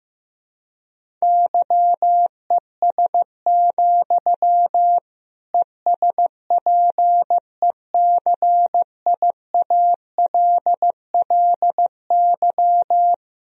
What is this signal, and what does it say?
Morse code, heard as a keyed tone, about 15 wpm, 700 Hz. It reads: YES, ESPECIALLY
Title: Sentences from Top 1000 Words - Review Part 4 - 15wpm